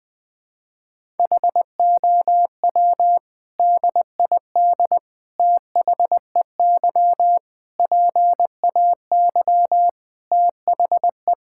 Morse code, heard as a keyed tone, about 20 wpm, 700 Hz. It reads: HOW DID THEY PAY THE
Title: Sentences from Top 1000 Words - Review Part 5 - 20wpm